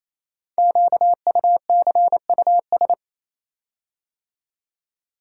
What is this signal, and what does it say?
Morse code, heard as a keyed tone, about 27 wpm, 700 Hz. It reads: QUCUS
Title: Code Groups - 5 Letters - 28wpm